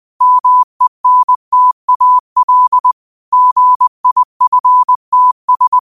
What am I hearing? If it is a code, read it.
Morse code, heard as a keyed tone, about 20 wpm, 1000 Hz. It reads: MENTAL GIFTS
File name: qotdfiles / 20wpm